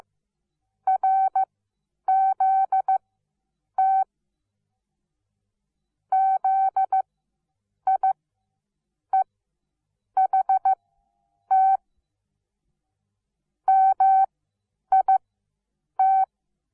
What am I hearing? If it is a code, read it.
Morse code, heard as a keyed tone, about 15 wpm, spaced as for 8 wpm, 770 Hz. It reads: RZT ZIEHT MIT
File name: Onlinelehrgang / morsen / DJ4UF Morsen17